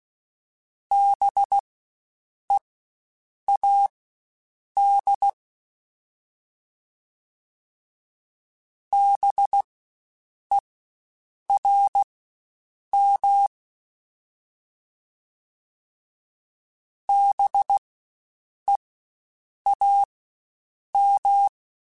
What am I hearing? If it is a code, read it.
Morse code, heard as a keyed tone, about 16 wpm, spaced as for 7 wpm, 800 Hz. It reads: BEAD BERM BEAM
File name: Spaced Alphas / Spaced Alphas mp3s / Bs sp @15